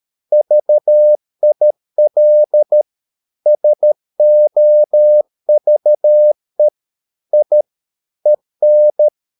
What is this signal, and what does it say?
Morse code, heard as a keyed tone, about 13 wpm, 600 Hz. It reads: VIL SOVE I EN
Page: Ferie 13 wpm